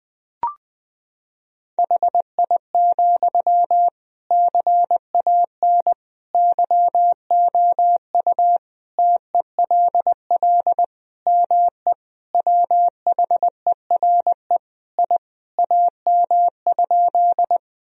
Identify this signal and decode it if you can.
Morse code, heard as a keyed tone, about 20 wpm, 700 Hz. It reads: HI, CAN YOU TELL ME WHERE I AM?
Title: Sentences from Top 1000 Words - Review Part 3 - 20wpm